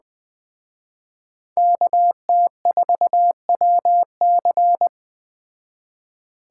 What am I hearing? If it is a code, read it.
Morse code, heard as a keyed tone, about 19 wpm, 700 Hz. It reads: KT4WC